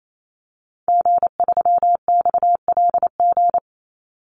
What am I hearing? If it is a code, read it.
Morse code, heard as a keyed tone, about 28 wpm, 700 Hz. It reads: G3XLG
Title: Call Signs - 28wpm